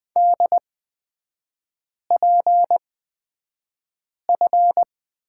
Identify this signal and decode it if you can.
Morse code, heard as a keyed tone, about 20 wpm, spaced as for 5 wpm, 700 Hz. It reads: DPF
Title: 243 Lesson 39e - Letter z - Three Characters 20wpm